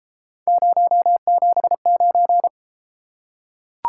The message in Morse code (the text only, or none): 079